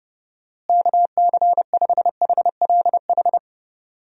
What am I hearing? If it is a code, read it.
Morse code, heard as a keyed tone, about 31 wpm, 700 Hz. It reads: KC5HLH
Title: USA Call Signs - 6 Characters - Repeated 2x - 30wpm